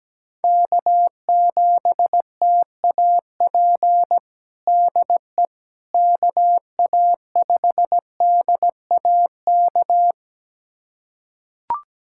Morse code, heard as a keyed tone, about 17 wpm, 700 Hz. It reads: K7TAP DE KA5DAK
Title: USA Callsign DE Callsign 17wpm